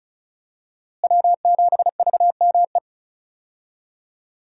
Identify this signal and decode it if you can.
Morse code, heard as a keyed tone, about 36 wpm, 700 Hz. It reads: W7VME
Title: USA Call Signs - 5 Characters - Repeated 3x - 35wpm